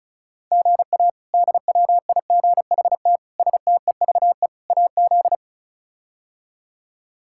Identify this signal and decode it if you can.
Morse code, heard as a keyed tone, about 36 wpm, 700 Hz. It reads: GA DWIGHT STEVE AZ
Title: K1USN SST QSO Practice - 35wpm